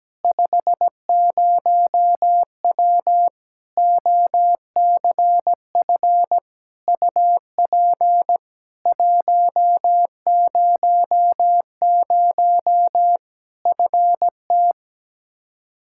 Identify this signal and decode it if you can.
Morse code, heard as a keyed tone, about 17 wpm, 700 Hz. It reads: 50W OCF UP 100 FT
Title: Practice QSO - Exchange 2 - Info - 17wpm